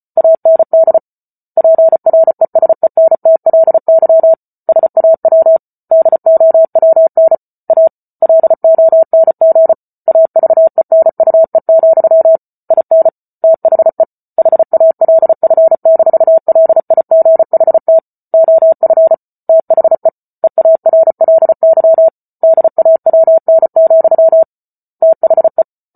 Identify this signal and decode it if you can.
Morse code, heard as a keyed tone, about 34 wpm, 650 Hz. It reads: AND PRESENTLY SAW DOWN A LONG AVENUE, IN THE HALF-LIGHT OF THE EARLY DAWN, THE
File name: Book Two / 08-Chapter 8 - 34 WPM